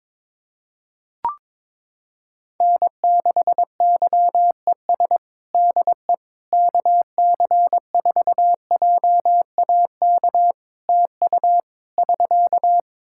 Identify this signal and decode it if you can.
Morse code, heard as a keyed tone, about 22 wpm, 700 Hz. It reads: N6YES DE KC4JAK TU <SK>
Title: K1USN SST QSO Practice 22wpm